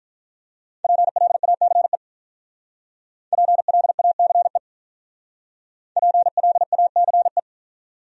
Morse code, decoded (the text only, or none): PLACE PLACE PLACE